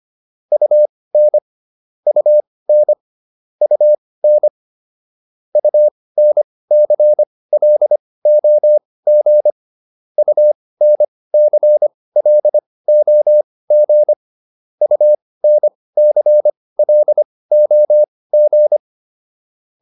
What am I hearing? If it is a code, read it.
Morse code, heard as a keyed tone, about 25 wpm, spaced as for 18 wpm, 600 Hz. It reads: UN UN UN UNCLOG UNCLOG UNCLOG